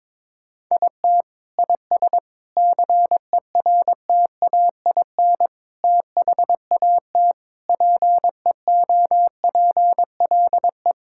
Morse code, heard as a keyed tone, about 22 wpm, 700 Hz. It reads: IT IS CERTAIN THAT PEOPLE
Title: Sentences from Top 1000 Words - Review Part 5 - 22wpm